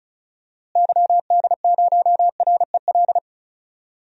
Morse code, heard as a keyed tone, about 36 wpm, 700 Hz. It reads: YD0REL
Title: DXCC-100 Call Signs - 6 Characters - Repeated 2x - 35wpm